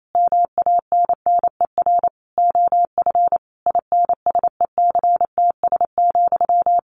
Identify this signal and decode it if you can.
Morse code, heard as a keyed tone, about 28 wpm, 700 Hz. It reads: MANNER OF INSECTS,
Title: A Sound Of Thunder 28wpm